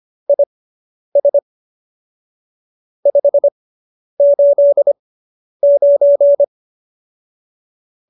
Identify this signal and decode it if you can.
Morse code, heard as a keyed tone, about 25 wpm, spaced as for 10 wpm, 570 Hz. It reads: IS 589